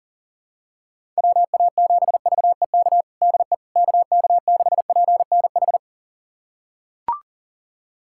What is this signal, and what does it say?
Morse code, heard as a keyed tone, about 37 wpm, 700 Hz. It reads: WA7VEK DE KK6PNH
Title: USA Callsign DE Callsign - 6 Characters - 40wpm